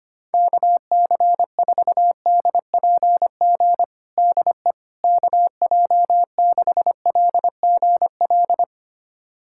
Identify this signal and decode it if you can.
Morse code, heard as a keyed tone, about 25 wpm, 700 Hz. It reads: KC4DPG DE KJ6LGL
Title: USA Callsign DE Callsign - 6 Characters - 25wpm